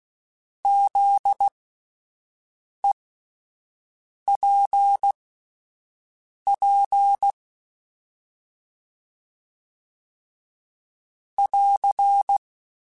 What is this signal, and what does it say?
Morse code, heard as a keyed tone, about 16 wpm, spaced as for 5 wpm, 800 Hz. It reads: ZEPP <AR>
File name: fromSMs / FromSMs mp3s / Antennas@5